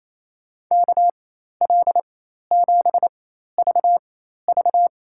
Morse code, heard as a keyed tone, about 28 wpm, spaced as for 13 wpm, 700 Hz. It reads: KL7VV
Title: USA Call Signs - 5 Characters - 4x Character Spacing - 28wpm